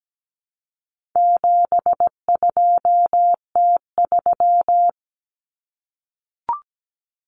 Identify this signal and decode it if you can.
Morse code, heard as a keyed tone, about 17 wpm, 700 Hz. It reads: 72T3
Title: Common Cut Numbers - 4-Digit 17wpm